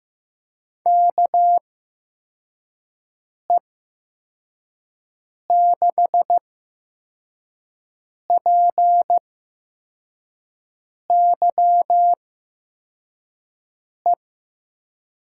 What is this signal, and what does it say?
Morse code, heard as a keyed tone, about 15 wpm, 700 Hz. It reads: K E 6 P Y E
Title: USA Call Signs - 6 Characters - 8x Character Spacing - 15wpm